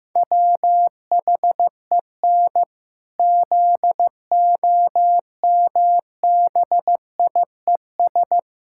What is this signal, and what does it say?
Morse code, heard as a keyed tone, about 15 wpm, 700 Hz. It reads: WHEN ZOMBIES
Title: Pangram Challenge v2 - 15wpm